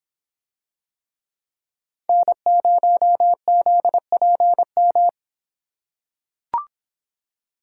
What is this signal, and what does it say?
Morse code, heard as a keyed tone, about 25 wpm, 700 Hz. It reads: N0ZPM